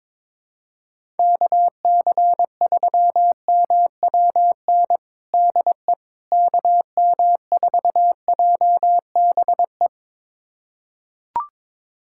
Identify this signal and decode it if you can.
Morse code, heard as a keyed tone, about 22 wpm, 700 Hz. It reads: KC3MWN DE KM4JBE
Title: USA Callsign DE Callsign - 6 Characters - 22wpm